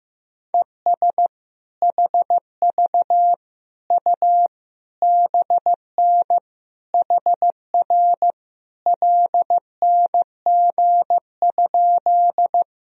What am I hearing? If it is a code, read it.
Morse code, heard as a keyed tone, about 16 wpm, 700 Hz. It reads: ES HV U BN HR LNG?
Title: Sentences from Top 200 Words and Abbreviations 15wpm